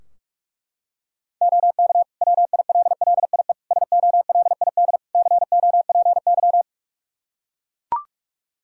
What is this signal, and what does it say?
Morse code, heard as a keyed tone, about 42 wpm, 700 Hz. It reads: OK WILLIE SOLID COPY
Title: Practice QSO - Exchange 3 - More Info - 45wpm